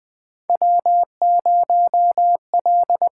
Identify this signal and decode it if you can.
Morse code, heard as a keyed tone, about 21 wpm, 700 Hz. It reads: W0L
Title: USA Call Signs - 4 Characters - 6x Character Spacing - 20wpm